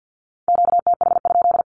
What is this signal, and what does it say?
Morse code, heard as a keyed tone, about 50 wpm, 700 Hz. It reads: QTH?
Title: Q-Codes with Spoken Meaning - 50wpm